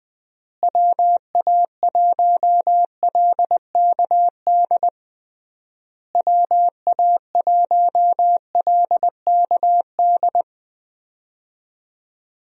Morse code, heard as a keyed tone, about 20 wpm, 700 Hz. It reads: WA1LKD WA1LKD
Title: USA Call Signs - 6 Characters - Repeated 3x - 20wpm